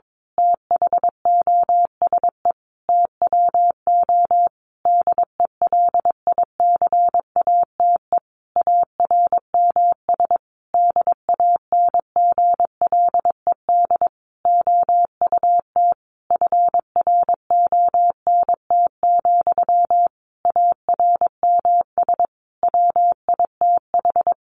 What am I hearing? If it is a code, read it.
Morse code, heard as a keyed tone, about 22 wpm, 700 Hz. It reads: THOSE TWO DELICATE ARMS DANGLED OUT FRONT, ARMS WITH
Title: A Sound Of Thunder 22wpm